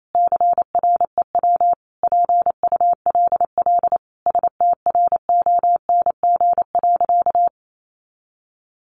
Morse code, heard as a keyed tone, about 28 wpm, 700 Hz. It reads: CREW PULL STRONG.
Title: Sentences from Top 1500 Words 28wpm